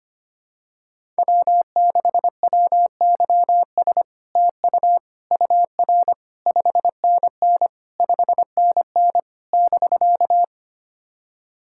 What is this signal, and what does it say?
Morse code, heard as a keyed tone, about 25 wpm, 700 Hz. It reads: W6WYS TU UR 5NN 5NN <BK>
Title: SOTA QSO Practice 25wpm